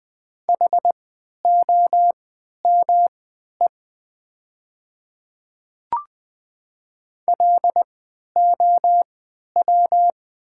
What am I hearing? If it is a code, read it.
Morse code, heard as a keyed tone, about 20 wpm, spaced as for 11 wpm, 700 Hz. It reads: HOME LOW
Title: 3x Character Spacing - Top 300 Words - 20wpm